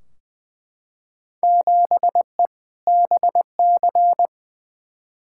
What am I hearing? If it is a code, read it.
Morse code, heard as a keyed tone, about 20 wpm, 700 Hz. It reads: 7E BC
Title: Field Day Exchanges Encore - 20wpm